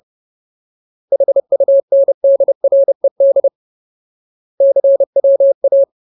Morse code, heard as a keyed tone, about 30 wpm, 550 Hz. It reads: HUNDRED CWA
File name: Ultimate Word List / 30WPM Ultimate